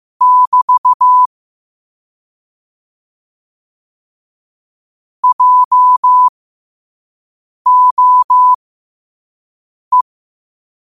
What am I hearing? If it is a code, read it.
Morse code, heard as a keyed tone, about 15 wpm, spaced as for 5 wpm, 1000 Hz. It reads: <BT> JOE
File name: qotdfiles / 5wpm